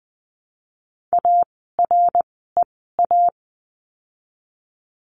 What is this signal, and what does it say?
Morse code, heard as a keyed tone, about 20 wpm, spaced as for 14 wpm, 700 Hz. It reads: AREA
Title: 115 Lesson 19f - Letter w - Words 20wpm